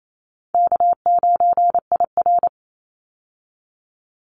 Morse code, heard as a keyed tone, about 28 wpm, 700 Hz. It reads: K9IR
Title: Call Signs - 28wpm